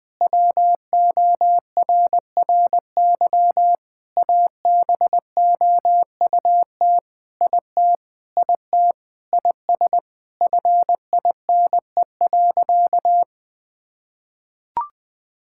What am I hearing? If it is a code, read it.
Morse code, heard as a keyed tone, about 19 wpm, 700 Hz. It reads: WORRY ABOUT IT IT IS FINE.
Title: Sentences with 5-Word Phrases - 20wpm